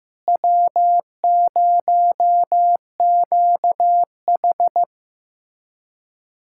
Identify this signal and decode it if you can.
Morse code, heard as a keyed tone, about 15 wpm, 700 Hz. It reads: W0QH
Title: USA Callsign DE Callsign - 4 Characters - 15wpm